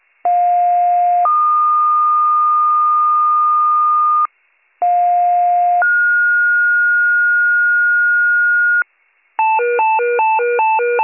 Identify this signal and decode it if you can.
Two-tone page, heard as a one-second tone, then a three-second tone, about 710 then 1220 Hz, about 710 then 1430 Hz, with noise. It is heard from 0.2 to 8.8 s.